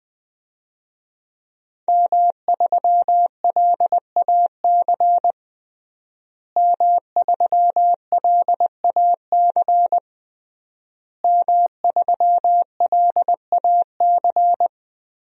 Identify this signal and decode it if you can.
Morse code, heard as a keyed tone, about 20 wpm, 700 Hz. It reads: M3LAC M3LAC M3LAC
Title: DXCC-100 Call Signs - 5 Characters - Repeated 3x - 20wpm